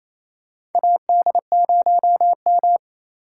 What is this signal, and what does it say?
Morse code, heard as a keyed tone, about 28 wpm, 700 Hz. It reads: AD0M
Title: USA Call Signs - 4 Characters - 4x Character Spacing - 28wpm